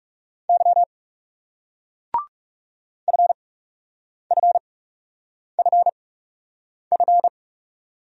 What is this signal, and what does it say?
Morse code, heard as a keyed tone, about 40 wpm, spaced as for 8 wpm, 700 Hz. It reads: Y FFFF